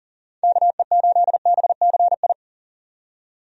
Morse code, heard as a keyed tone, about 37 wpm, 700 Hz. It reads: KE8BCI
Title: USA Call Signs - 6 Characters - 40wpm